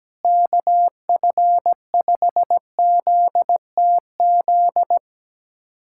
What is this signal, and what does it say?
Morse code, heard as a keyed tone, about 17 wpm, 700 Hz. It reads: KF5ZTZ
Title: USA Callsign DE Callsign 17wpm